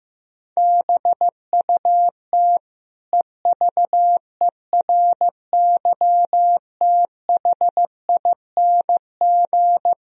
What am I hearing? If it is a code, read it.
Morse code, heard as a keyed tone, about 15 wpm, 700 Hz. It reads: BUT EVERYTHING
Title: Sentences from Top 1000 Words - Review Part 2 - 15wpm